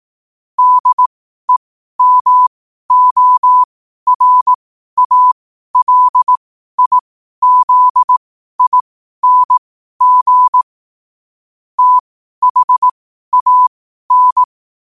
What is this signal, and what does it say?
Morse code, heard as a keyed tone, about 18 wpm, spaced as for 12 wpm, 1000 Hz. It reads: DEMORALIZING THAN
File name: qotdfiles / 12wpm